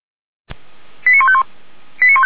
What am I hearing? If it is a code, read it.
Five-tone selective call (ZVEI), heard as quick runs of five tones, tones of about 70 ms. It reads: 89151